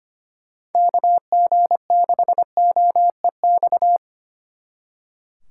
Morse code, heard as a keyed tone, about 31 wpm, 700 Hz. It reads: KG6OEX
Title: Speed Racing - USA 2x3 Call Signs 25wpm